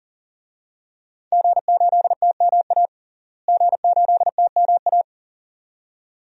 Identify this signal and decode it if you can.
Morse code, heard as a keyed tone, about 40 wpm, 700 Hz. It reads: G8TMA G8TMA
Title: DXCC-100 Call Signs - 5 Characters - Repeated 2x - 40wpm